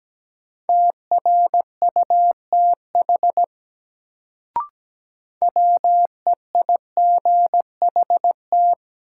Morse code, heard as a keyed tone, about 17 wpm, 700 Hz. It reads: TRUTH WEIGHT
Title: Sentences from Top 800 Words - New Word Drill - 17wpm